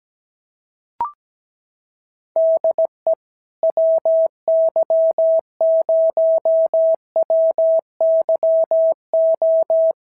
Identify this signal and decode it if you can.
Morse code, heard as a keyed tone, about 17 wpm, 650 Hz. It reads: DE WY0WYO
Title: SOTA QSO Practice 17wpm